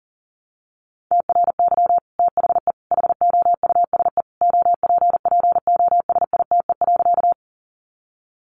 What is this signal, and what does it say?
Morse code, heard as a keyed tone, about 40 wpm, 700 Hz. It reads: TRY THE HOUSE OPPOSITE.
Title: Sentences from Top 1300 Words 40wpm